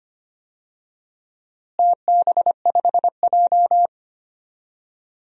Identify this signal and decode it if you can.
Morse code, heard as a keyed tone, about 25 wpm, 700 Hz. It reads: TB5J
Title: DXCC-100 Call Signs - 4 Characters - Repeated 2x - 25wpm